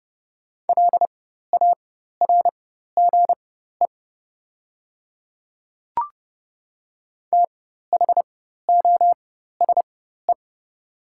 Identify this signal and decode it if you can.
Morse code, heard as a keyed tone, about 31 wpm, spaced as for 14 wpm, 700 Hz. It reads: LARGE THOSE
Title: 4x Character Spacing - Top 300 Words - 30wpm